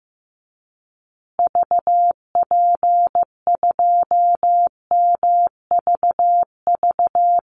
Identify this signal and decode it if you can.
Morse code, heard as a keyed tone, about 15 wpm, 700 Hz. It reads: VP2MVV
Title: Call Signs Repeated 2x - 15wpm